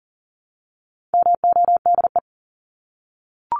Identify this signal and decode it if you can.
Morse code, heard as a keyed tone, about 40 wpm, 700 Hz. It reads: MODE
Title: Top 1751-2000 Words 40wpm